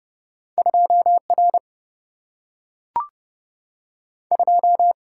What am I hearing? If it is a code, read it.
Morse code, heard as a keyed tone, about 30 wpm, 700 Hz. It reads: 2 R 2
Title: Single-Letter Single-Number 30wpm